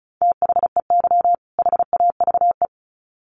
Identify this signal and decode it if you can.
Morse code, heard as a keyed tone, about 34 wpm, 700 Hz. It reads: THEY HAVE
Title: Sets of 2 Words 35wpm